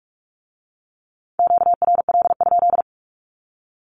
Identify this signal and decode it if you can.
Morse code, heard as a keyed tone, about 46 wpm, 700 Hz. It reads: QRL?
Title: Q-Codes with Spoken Meaning - 45wpm